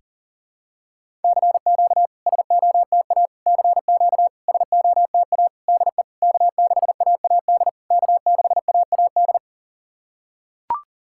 Morse code, heard as a keyed tone, about 43 wpm, 700 Hz. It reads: CQ SOTA CQ SOTA DE K6AAD K6AAD
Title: SOTA QSO Practice 40wpm